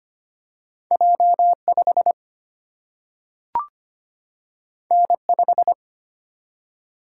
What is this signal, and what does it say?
Morse code, heard as a keyed tone, about 25 wpm, 700 Hz. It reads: J5 N5
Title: Single-Letter Single-Number 25wpm